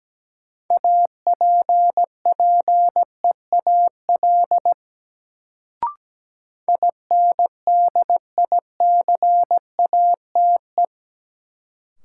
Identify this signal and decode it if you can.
Morse code, heard as a keyed tone, about 17 wpm, 700 Hz. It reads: APPEAL INDICATE
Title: Sentences from Top 1300 Words - New Word Drill Alt - 17wpm